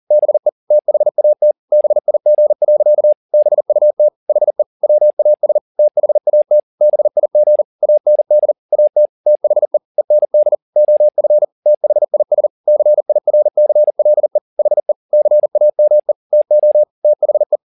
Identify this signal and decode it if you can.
Morse code, heard as a keyed tone, about 40 wpm, 600 Hz. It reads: BE THAT BIG. BUT HE WAS THAT BIG AND AT THE END OF THIS CIRCLE HE CAME TO THE
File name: the old man and the sea 2 40wpm